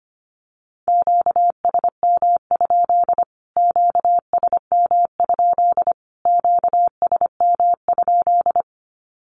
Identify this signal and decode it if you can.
Morse code, heard as a keyed tone, about 25 wpm, 700 Hz. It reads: QSM? QSM? QSM?
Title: Q-Codes Repeated 3x 25wpm